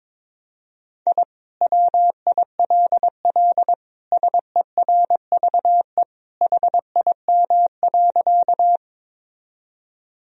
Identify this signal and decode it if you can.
Morse code, heard as a keyed tone, about 22 wpm, 700 Hz. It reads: I WILL SERVE HIM.
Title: Sentences from Top 1000 Words - Review Part 2 - 22wpm